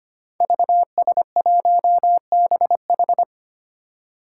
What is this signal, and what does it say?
Morse code, heard as a keyed tone, about 25 wpm, 700 Hz. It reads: VS1BH
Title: Five Challenging Letter-Number Combinations 25wpm